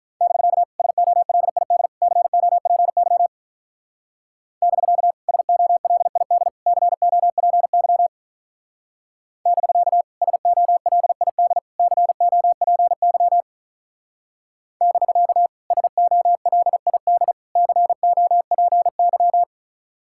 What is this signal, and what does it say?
Morse code, heard as a keyed tone, about 47 wpm, 700 Hz. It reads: <BK> SOLID COPY <BK> SOLID COPY <BK> SOLID COPY <BK> SOLID COPY